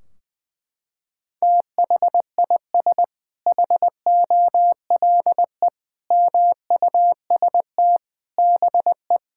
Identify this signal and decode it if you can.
Morse code, heard as a keyed tone, about 19 wpm, 700 Hz. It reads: THIS HOLE MUST BE
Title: Sentences from Top 1600 Words 20wpm